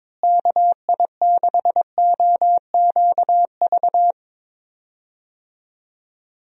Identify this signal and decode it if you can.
Morse code, heard as a keyed tone, about 22 wpm, 700 Hz. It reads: KI6OQV
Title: USA Call Signs - 6 Characters - Repeated 2x - 22wpm